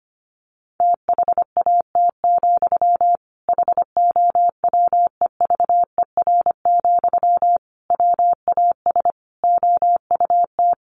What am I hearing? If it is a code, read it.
Morse code, heard as a keyed tone, about 25 wpm, 700 Hz. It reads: THAT, HOWEVER, WAS OUT
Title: Sentences from Top 600 Words 25wpm